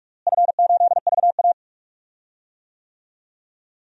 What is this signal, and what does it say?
Morse code, heard as a keyed tone, about 43 wpm, 700 Hz. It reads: F8VA
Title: DXCC-100 Call Signs - 4 Characters - Repeated 2x - 45wpm